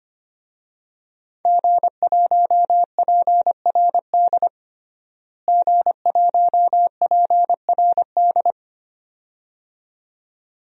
Morse code, heard as a keyed tone, about 25 wpm, 700 Hz. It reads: G1PRD G1PRD
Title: DXCC-100 Call Signs - 5 Characters - Repeated 2x - 25wpm